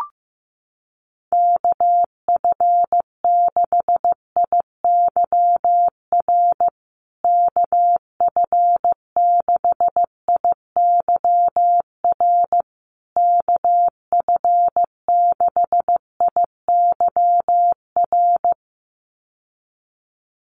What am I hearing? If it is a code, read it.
Morse code, heard as a keyed tone, about 15 wpm, 700 Hz. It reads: KF6IYR KF6IYR KF6IYR
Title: USA Call Signs Repeated 3x 15wpm